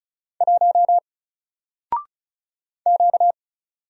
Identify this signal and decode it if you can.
Morse code, heard as a keyed tone, about 36 wpm, 700 Hz. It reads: 1 Q
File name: Single-Letter-Number-35wpm